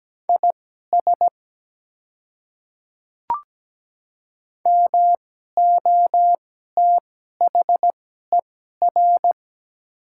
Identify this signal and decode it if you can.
Morse code, heard as a keyed tone, about 17 wpm, spaced as for 12 wpm, 700 Hz. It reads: IS MOTHER
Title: 2x Character Spacing - Top 200 Words - 17wpm